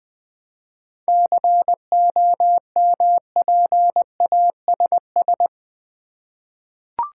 Morse code, heard as a keyed tone, about 20 wpm, 700 Hz. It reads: COMPASS
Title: Easy Words with Hints - 20wpm